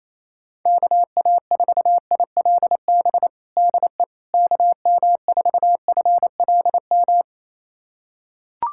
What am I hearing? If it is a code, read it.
Morse code, heard as a keyed tone, about 28 wpm, 700 Hz. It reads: KA4ILB DE KM4FLM
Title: USA Callsign DE Callsign 28wpm